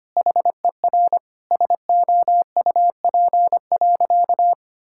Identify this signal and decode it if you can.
Morse code, heard as a keyed tone, about 25 wpm, 700 Hz. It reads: HER SOUP.
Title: Sentences from Top 1000 Words - Review Part 6 - 25wpm